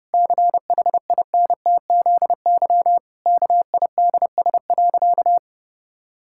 Morse code, heard as a keyed tone, about 30 wpm, 700 Hz. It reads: CHINTZY KIDS.
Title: Pangram Challenge v2 - 30wpm